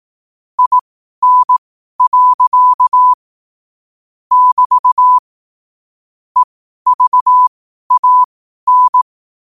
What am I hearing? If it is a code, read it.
Morse code, heard as a keyed tone, about 18 wpm, spaced as for 12 wpm, 1000 Hz. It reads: IN. <BT> EVAN
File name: qotdfiles / 12wpm